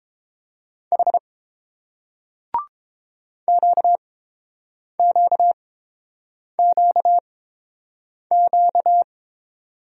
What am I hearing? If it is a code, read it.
Morse code, heard as a keyed tone, about 30 wpm, spaced as for 7 wpm, 690 Hz. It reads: H QQQQ